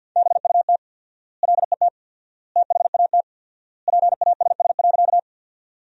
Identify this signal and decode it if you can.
Morse code, heard as a keyed tone, about 47 wpm, 700 Hz. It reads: BUT LET THAT PASS.
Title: 4x Word-Spacing - Sentences from Top 400 Words - 50wpm